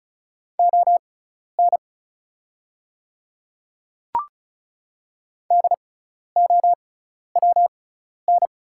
Morse code, heard as a keyed tone, about 36 wpm, spaced as for 12 wpm, 700 Hz. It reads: ON DOWN
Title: 6x Character Spacing - Top 200 Words - 35wpm